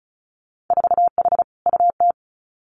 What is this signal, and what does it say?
Morse code, heard as a keyed tone, about 36 wpm, 700 Hz. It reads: 4H UT
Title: Winter Field Day Exchanges Encore 35wpm